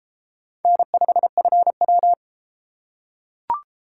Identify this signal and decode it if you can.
Morse code, heard as a keyed tone, about 30 wpm, 700 Hz. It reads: N5FW